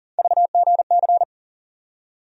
Morse code, heard as a keyed tone, about 40 wpm, 700 Hz. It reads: VGC
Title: USA Call Sign Postfixes - 3 Letters 40wpm